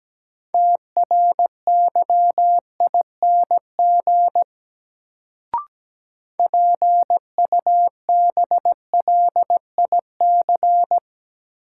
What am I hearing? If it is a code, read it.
Morse code, heard as a keyed tone, about 17 wpm, 700 Hz. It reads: TRYING PUBLIC
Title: Sets of 2 Words - New Word Drill - 17wpm